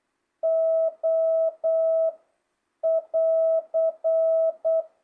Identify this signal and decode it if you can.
Morse code, heard as a keyed tone, about 8 wpm, 660 Hz. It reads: O <AR>
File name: jh8cbh / 2025cw